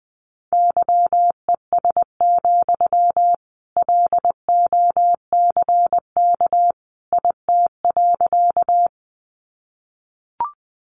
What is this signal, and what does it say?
Morse code, heard as a keyed tone, about 20 wpm, 700 Hz. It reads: YES, LOCK IT.
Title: Sentences from Top 1300 Words 20wpm